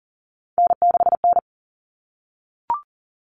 Morse code, heard as a keyed tone, about 40 wpm, 700 Hz. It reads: N6N
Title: Common Cut Numbers - 3-Digit 40wpm